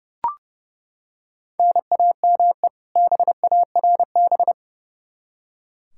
Morse code, heard as a keyed tone, about 32 wpm, 700 Hz. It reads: NAME BARB
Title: K1USN SST QSO Practice 30wpm